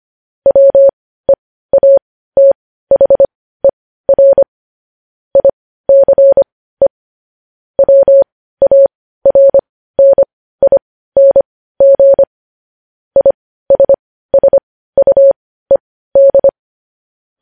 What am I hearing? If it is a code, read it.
Morse code, heard as a keyed tone, about 25 wpm, spaced as for 15 wpm, 550 Hz. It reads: WEATHER ICE WARNING ISSUED
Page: news-headlines-in-morse-code-at-15-wpm